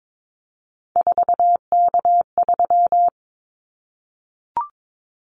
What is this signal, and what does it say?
Morse code, heard as a keyed tone, about 22 wpm, 700 Hz. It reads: 4K3
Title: Callsign Prefixes for Top 100 DXCC - 22wpm